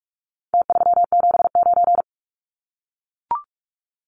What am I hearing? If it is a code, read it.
Morse code, heard as a keyed tone, about 44 wpm, 700 Hz. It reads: T379
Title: Numbers - Contest Style - 4-Digit 45wpm